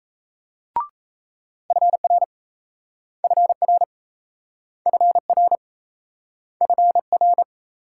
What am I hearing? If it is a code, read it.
Morse code, heard as a keyed tone, about 37 wpm, 700 Hz. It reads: FR FR FR FR